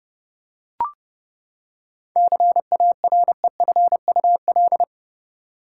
Morse code, heard as a keyed tone, about 32 wpm, 700 Hz. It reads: CAREFUL
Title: Seven Letter Words - 30wpm